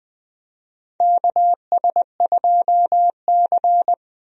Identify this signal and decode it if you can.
Morse code, heard as a keyed tone, about 20 wpm, 700 Hz. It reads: KS2C
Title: USA Call Signs - 4 Characters - Repeated 2x - 20wpm